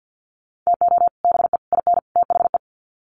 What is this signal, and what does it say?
Morse code, heard as a keyed tone, about 50 wpm, 700 Hz. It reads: TO BE IN THE
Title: Sets of 4 Words 50wpm